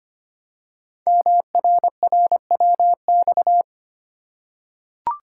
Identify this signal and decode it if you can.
Morse code, heard as a keyed tone, about 25 wpm, 700 Hz. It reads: MRRWX
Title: Code Groups - 5 Letters - 25wpm